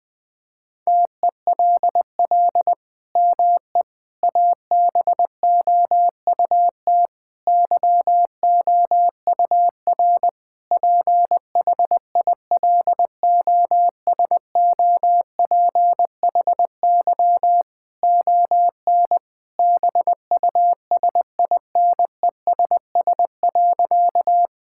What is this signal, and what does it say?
Morse code, heard as a keyed tone, about 21 wpm, 700 Hz. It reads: TELL ME ABOUT YOUR PHILOSOPHY ON BUSINESS.
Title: Sentences from Top 1000 Words - Review Part 6 - 20wpm